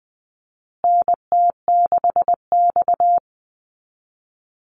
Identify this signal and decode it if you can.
Morse code, heard as a keyed tone, about 20 wpm, 700 Hz. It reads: NT6X
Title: Call Signs Repeated 2x - 20wpm